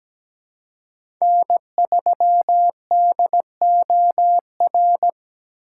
Morse code, heard as a keyed tone, about 17 wpm, 700 Hz. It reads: N3DOR
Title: USA Call Signs - 5 Characters - 3x Character Spacing - 17wpm